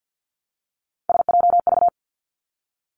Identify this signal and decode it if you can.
Morse code, heard as a keyed tone, about 50 wpm, 700 Hz. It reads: SJV
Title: ARRL Sections 50wpm